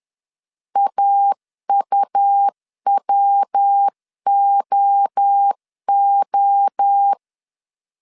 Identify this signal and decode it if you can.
Morse code, heard as a keyed tone, about 10 wpm, 800 Hz. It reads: AUWOO
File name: 10wpm